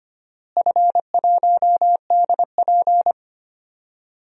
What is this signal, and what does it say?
Morse code, heard as a keyed tone, about 25 wpm, 700 Hz. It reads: F1DP
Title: DXCC-100 Call Signs - 4 Characters - Repeated 2x - 25wpm